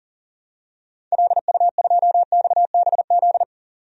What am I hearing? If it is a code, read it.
Morse code, heard as a keyed tone, about 37 wpm, 700 Hz. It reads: LU2XBZ
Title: DXCC-100 Call Signs - 6 Characters - 40wpm